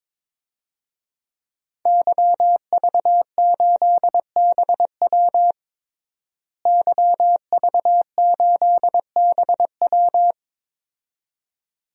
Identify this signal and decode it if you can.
Morse code, heard as a keyed tone, about 22 wpm, 700 Hz. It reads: YV8BW YV8BW
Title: DXCC-100 Call Signs - 5 Characters - Repeated 2x - 22wpm